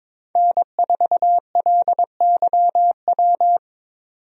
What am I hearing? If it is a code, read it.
Morse code, heard as a keyed tone, about 22 wpm, 700 Hz. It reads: N4LYW
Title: USA Call Signs - 5 Characters - 22wpm